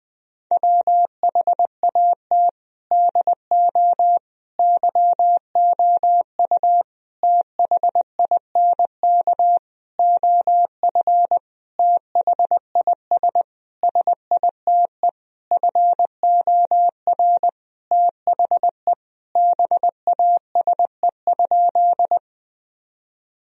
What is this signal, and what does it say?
Morse code, heard as a keyed tone, about 20 wpm, 700 Hz. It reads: WHAT DO YOU THINK OF THIS SITE FOR THE BASE?
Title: Sentences from Top 1000 Words - Review Part 1 - 20wpm